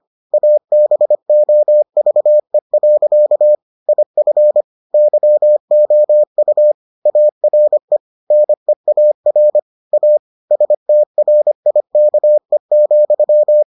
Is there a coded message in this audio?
Yes: Morse code from 0.3 to 13.6 s, heard as a keyed tone, about 25 wpm, 600 Hz. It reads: ABOVE. IF YOU ARE NEAR A STRIKE,